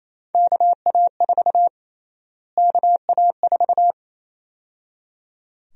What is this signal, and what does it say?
Morse code, heard as a keyed tone, about 28 wpm, 700 Hz. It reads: KA4 KA4
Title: USA Call Sign Prefixes - Repeated 2x - 28wpm